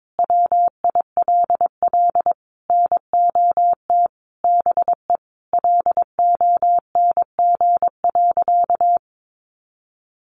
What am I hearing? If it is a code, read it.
Morse code, heard as a keyed tone, about 22 wpm, 700 Hz. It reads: WILL NOT BE LONG.
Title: Sentences from Top 1200 Words 22wpm